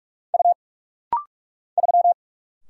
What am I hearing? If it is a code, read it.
Morse code, heard as a keyed tone, about 46 wpm, 700 Hz. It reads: U 3
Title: Single Letter-Number Alt - Rapid-Fire 45wpm